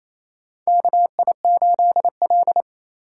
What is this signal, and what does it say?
Morse code, heard as a keyed tone, about 28 wpm, 690 Hz. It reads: KI8L
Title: USA Call Signs - 2x Character Spacing - 28wpm